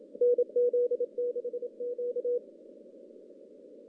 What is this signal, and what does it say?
Morse code, heard as a keyed tone, about 29 wpm, 490 Hz. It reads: NZ6Q
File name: contests / CW-SS-k5tr- -2300